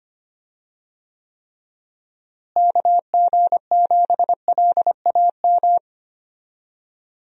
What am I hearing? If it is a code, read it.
Morse code, heard as a keyed tone, about 25 wpm, 700 Hz. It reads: KG7LAM
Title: USA Call Signs - 6 Characters - Repeated 3x - 25wpm